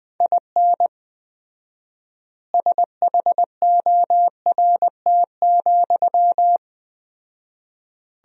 Morse code, heard as a keyed tone, about 21 wpm, 700 Hz. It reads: IN SHORT,
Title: 4x Word-Spacing - Sentences from Top 400 Words - 20wpm